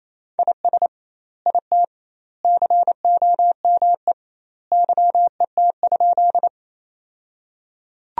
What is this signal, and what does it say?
Morse code, heard as a keyed tone, about 28 wpm, 700 Hz. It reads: IS IT COME YET?
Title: 2x Word-Spacing - Sentences from Top 500 Words - 28wpm